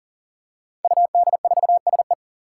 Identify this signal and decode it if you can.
Morse code, heard as a keyed tone, about 39 wpm, 700 Hz. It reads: UD4SE
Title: Five Challenging Letter-Number Combinations 40wpm